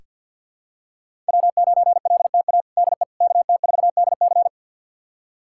Morse code, heard as a keyed tone, about 46 wpm, 700 Hz. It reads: W9LTA DE KT4DC
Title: USA Callsign DE Callsign - 5 Characters - 50wpm